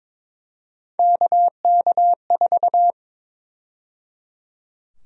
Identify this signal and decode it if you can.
Morse code, heard as a keyed tone, about 21 wpm, 700 Hz. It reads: KK4
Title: USA Call Sign Prefixes - 22wpm